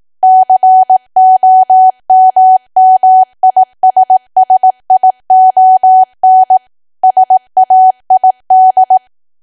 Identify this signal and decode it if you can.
Morse code, heard as a keyed tone, about 18 wpm, 750 Hz. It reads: COMMISSION SAID
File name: cwa / W1AW-Level2 / 18WPM